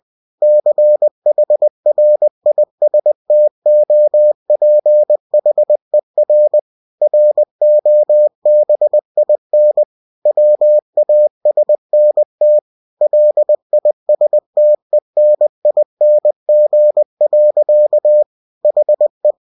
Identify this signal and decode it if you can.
Morse code, heard as a keyed tone, about 20 wpm, 600 Hz. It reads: CHRISTOPHER ROBIN WASNT LISTENING. HE
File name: cw-books / winnie-the-pooh / winnie-20wpm-page-0087